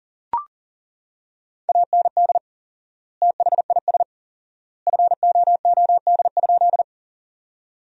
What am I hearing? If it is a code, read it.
Morse code, heard as a keyed tone, about 40 wpm, 700 Hz. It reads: AND THIS FOOD?
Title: 4x Word-Spacing - Sentences from Top 300 Words - 40wpm